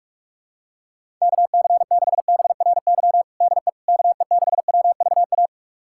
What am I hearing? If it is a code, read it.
Morse code, heard as a keyed tone, about 41 wpm, 700 Hz. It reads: KC6BRY DE KE6WVA
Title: USA Callsign DE Callsign - 6 Characters - 45wpm